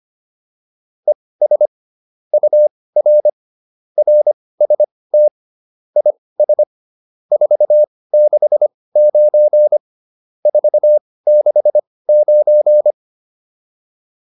Morse code, heard as a keyed tone, about 25 wpm, spaced as for 18 wpm, 600 Hz. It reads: ES UR RST IS 469 469